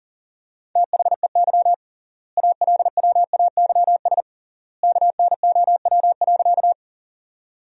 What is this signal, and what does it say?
Morse code, heard as a keyed tone, about 40 wpm, 700 Hz. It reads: THEY ALWAYS KNOW.
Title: 3x Word-Spacing - Sentences from Top 300 Words - 40wpm